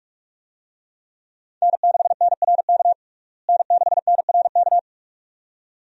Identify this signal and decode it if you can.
Morse code, heard as a keyed tone, about 46 wpm, 700 Hz. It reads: N6NRK N6NRK
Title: USA Call Signs - 5 Characters - Repeated 2x - 45wpm